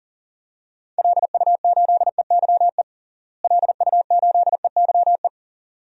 Morse code, heard as a keyed tone, about 40 wpm, 700 Hz. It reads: LU8EYE LU8EYE
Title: DXCC-100 Call Signs - 6 Characters - Repeated 2x - 40wpm